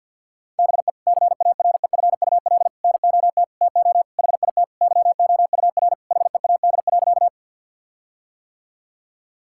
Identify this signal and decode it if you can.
Morse code, heard as a keyed tone, about 50 wpm, 700 Hz. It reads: BE CAREFUL NOT TO HIT YOUR HEAD.
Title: Sentences from Top 1600 Words 50wpm